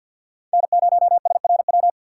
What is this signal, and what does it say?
Morse code, heard as a keyed tone, about 46 wpm, 700 Hz. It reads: N0SRW
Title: USA Call Signs - 5 Characters - 6x Character Spacing - 50wpm